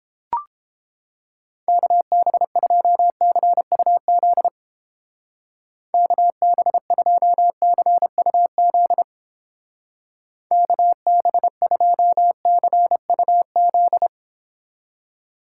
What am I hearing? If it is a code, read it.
Morse code, heard as a keyed tone, about 31 wpm, 700 Hz. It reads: KB2CUZ KB2CUZ KB2CUZ